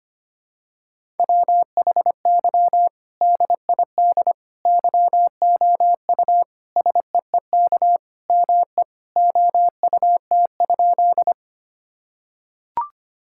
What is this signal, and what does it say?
Morse code, heard as a keyed tone, about 25 wpm, 700 Hz. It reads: WHY DID YOU SEEK ME OUT?
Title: Sentences from Top 1600 Words 25wpm